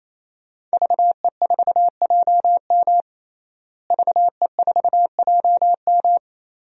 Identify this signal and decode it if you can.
Morse code, heard as a keyed tone, about 28 wpm, 700 Hz. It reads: VE4JM VE4JM
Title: DXCC-100 Call Signs - 5 Characters - Repeated 3x - 28wpm